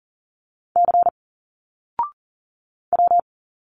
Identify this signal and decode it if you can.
Morse code, heard as a keyed tone, about 40 wpm, 700 Hz. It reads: C W
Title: Single Letters 40wpm